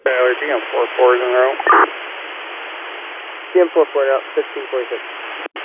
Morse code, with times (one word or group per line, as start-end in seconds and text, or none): none